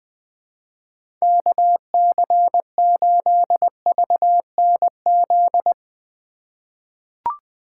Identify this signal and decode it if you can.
Morse code, heard as a keyed tone, about 20 wpm, 700 Hz. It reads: KC8VNZ
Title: USA Call Signs - 3x Character Spacing - 20wpm